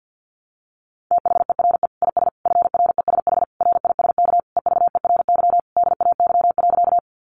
Morse code, heard as a keyed tone, about 50 wpm, 700 Hz. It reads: THERE IS FRESH RISK EVERY DAY.
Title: Sentences from Top 1200 Words 50wpm